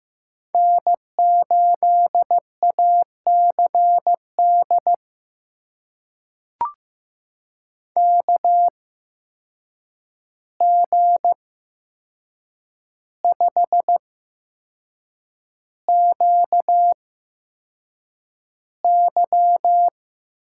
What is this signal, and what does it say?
Morse code, heard as a keyed tone, about 15 wpm, spaced as for 6 wpm, 700 Hz. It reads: N8ACD KG5QY
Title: USA Call Signs - 5 Characters - 8x Character Spacing - 15wpm